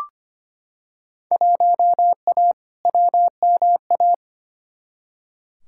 Morse code, heard as a keyed tone, about 25 wpm, 700 Hz. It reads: 1A WMA
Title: Field Day Exchanges - 25wpm